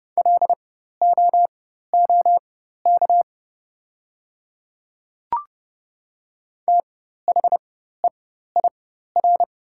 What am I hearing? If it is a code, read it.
Morse code, heard as a keyed tone, about 31 wpm, spaced as for 14 wpm, 690 Hz. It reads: LOOK THEIR
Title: 4x Character Spacing - Top 300 Words - 30wpm